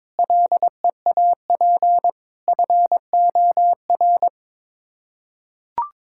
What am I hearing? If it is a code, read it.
Morse code, heard as a keyed tone, about 22 wpm, 700 Hz. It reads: LEAP FOR
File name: Sets of 9 Words - 22wpm